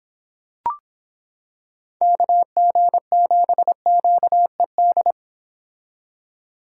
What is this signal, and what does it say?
Morse code, heard as a keyed tone, about 26 wpm, 700 Hz. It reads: KG7QED